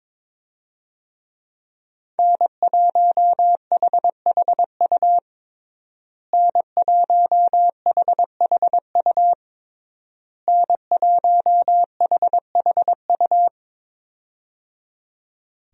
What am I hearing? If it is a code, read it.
Morse code, heard as a keyed tone, about 22 wpm, 700 Hz. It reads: N1HHU N1HHU N1HHU
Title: USA Call Signs - 5 Characters - Repeated 3x - 22wpm